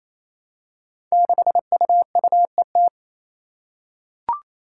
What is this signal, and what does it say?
Morse code, heard as a keyed tone, about 28 wpm, 700 Hz. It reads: 6UUET
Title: Five Challenging Letter-Number Combinations 28wpm